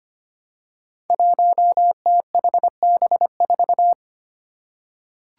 Morse code, heard as a keyed tone, about 25 wpm, 700 Hz. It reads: 1THB4
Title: Five Challenging Letter-Number Combinations 25wpm